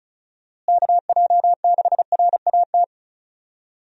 Morse code, heard as a keyed tone, about 36 wpm, 700 Hz. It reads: KJ6RAT
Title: USA Call Signs - 6 Characters - Repeated 3x - 35wpm